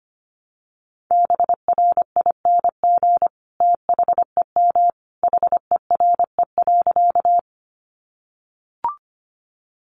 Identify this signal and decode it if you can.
Morse code, heard as a keyed tone, about 25 wpm, 700 Hz. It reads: BRING THEM HERE.
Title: Sentences from Top 500 Words 25wpm